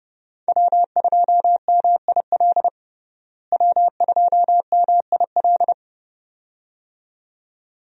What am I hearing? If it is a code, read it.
Morse code, heard as a keyed tone, about 30 wpm, 700 Hz. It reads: W2MIL W2MIL
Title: USA Call Signs - 5 Characters - Repeated 2x - 30wpm